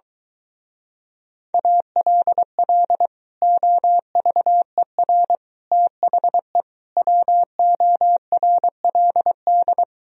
Morse code, heard as a keyed tone, about 23 wpm, 700 Hz. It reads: ALL OVER THE WORLD